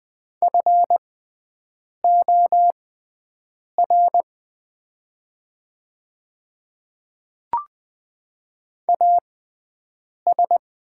Morse code, heard as a keyed tone, about 20 wpm, spaced as for 7 wpm, 700 Hz. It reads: FOR AS
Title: 6x Character Spacing - Top 100 Words - 20wpm